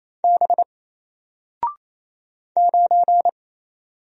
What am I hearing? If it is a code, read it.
Morse code, heard as a keyed tone, about 27 wpm, 700 Hz. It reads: B 9
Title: Single Letter-Number 28wpm